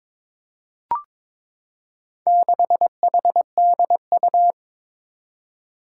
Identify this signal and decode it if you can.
Morse code, heard as a keyed tone, about 22 wpm, 700 Hz. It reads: 6HDU
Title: Four Challenging Letter-Number Combinations 22wpm